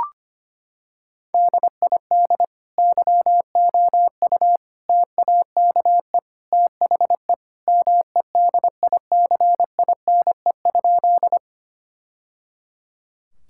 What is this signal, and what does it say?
Morse code, heard as a keyed tone, about 25 wpm, 700 Hz. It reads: DID YOU TAKE THE MEDICINE?
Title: Sentences from Top 1000 Words - Review Part 6 - 25wpm